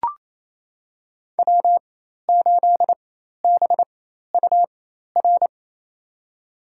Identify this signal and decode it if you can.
Morse code, heard as a keyed tone, about 28 wpm, spaced as for 13 wpm, 700 Hz. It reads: W8BUR
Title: USA Call Signs - 5 Characters - 4x Character Spacing - 28wpm